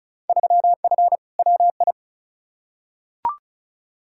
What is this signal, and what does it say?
Morse code, heard as a keyed tone, about 36 wpm, 700 Hz. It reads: 3F WI
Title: Field Day Exchanges - 35wpm